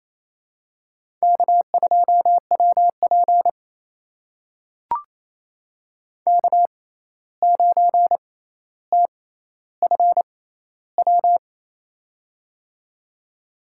Morse code, heard as a keyed tone, about 28 wpm, spaced as for 13 wpm, 700 Hz. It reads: K2WP K9TFW
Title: USA Call Signs - 6x Character Spacing - 28wpm